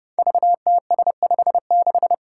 Morse code, heard as a keyed tone, about 28 wpm, 700 Hz. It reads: VTS56
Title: Five Challenging Letter-Number Combinations 30wpm